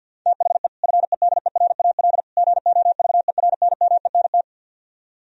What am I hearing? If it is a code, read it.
Morse code, heard as a keyed tone, about 64 wpm, 700 Hz. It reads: THE FEDERAL GOVERNMENT